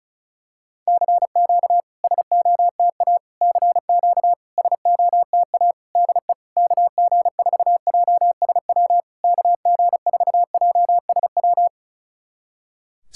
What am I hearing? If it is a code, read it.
Morse code, heard as a keyed tone, about 36 wpm, 690 Hz. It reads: CQ SOTA CQ SOTA DE KG4JSW KG4JSW
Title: SOTA QSO Practice 35wpm